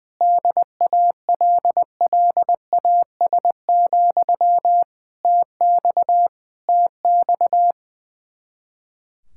Morse code, heard as a keyed tone, about 20 wpm, 700 Hz. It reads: DALLAS, TX TX
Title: Speed-Racing Common QSO Phrases Encore - 20wpm